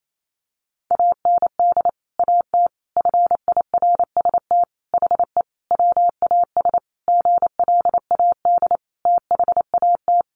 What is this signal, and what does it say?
Morse code, heard as a keyed tone, about 28 wpm, 700 Hz. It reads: AND AT FIRST HE WAS GLAD THAT
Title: Sentences from Top 1300 Words 28wpm